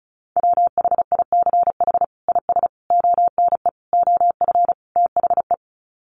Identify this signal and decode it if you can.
Morse code, heard as a keyed tone, about 34 wpm, 700 Hz. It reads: WHICH IS ONE OF THE
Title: Sets of 5 Words Encore 35wpm